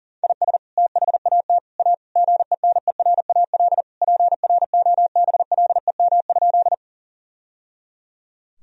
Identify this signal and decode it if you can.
Morse code, heard as a keyed tone, about 40 wpm, 700 Hz. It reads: IS THAT A GENERAL PROBLEM?
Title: Sentences from Top 1000 Words - Review Part 3 - 40wpm